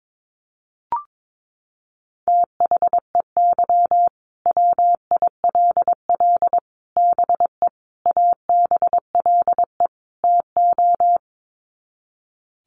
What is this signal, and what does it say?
Morse code, heard as a keyed tone, about 22 wpm, 700 Hz. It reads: THEY WILL BE ABLE TO
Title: Sets of 5 Words Encore 22wpm